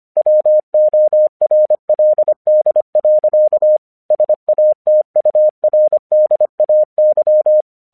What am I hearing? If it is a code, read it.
Morse code, heard as a keyed tone, about 24 wpm, 610 Hz. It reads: WORLD. SATURDAY
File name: 1 English 25wpm / AQC13